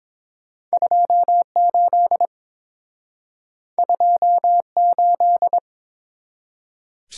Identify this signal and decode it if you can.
Morse code, heard as a keyed tone, about 30 wpm, 700 Hz. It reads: 28 28